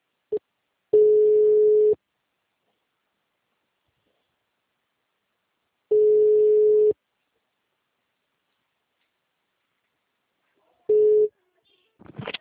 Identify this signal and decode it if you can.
Ringback tone (Europe, 425 Hz).
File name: out- -2029- - - .10